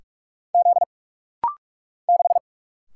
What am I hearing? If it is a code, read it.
Morse code, heard as a keyed tone, about 48 wpm, 690 Hz. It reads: Z 6
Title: Single Letter-Number - Rapid-Fire 45wpm